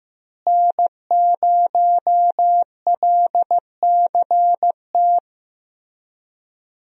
Morse code, heard as a keyed tone, about 15 wpm, 700 Hz. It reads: N0LCT
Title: USA Call Signs - 5 Characters - Repeated 2x - 15wpm